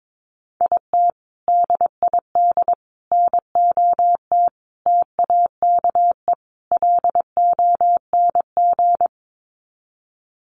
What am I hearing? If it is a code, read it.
Morse code, heard as a keyed tone, about 22 wpm, 700 Hz. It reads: IT DID NOT TAKE LONG
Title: Sets of 5 Words Encore2 22wpm